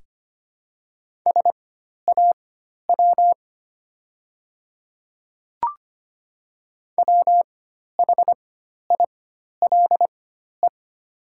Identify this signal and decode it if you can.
Morse code, heard as a keyed tone, about 25 wpm, spaced as for 12 wpm, 700 Hz. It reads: SAW WHILE
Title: 4x Character Spacing - Top 300 Words - 25wpm